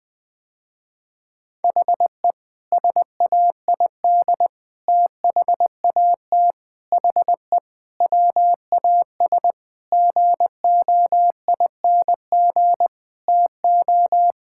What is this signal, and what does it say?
Morse code, heard as a keyed tone, about 21 wpm, 700 Hz. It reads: HE SAID THAT HE WAS GOING TO
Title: Sentences with 5-Word Phrases - Spoken Beginning - 20wpm